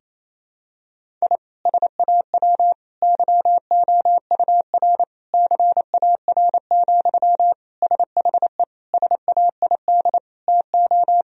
Morse code, heard as a keyed tone, about 28 wpm, 700 Hz. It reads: I SAW YOUR CAR, SHE SAID TO
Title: Sentences from Top 1000 Words - Review Part 1 - 28wpm